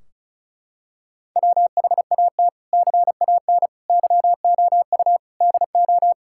Morse code, heard as a keyed tone, about 34 wpm, 700 Hz. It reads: WHAT CAN YOU DO
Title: Sentences from Top 1600 Words 35wpm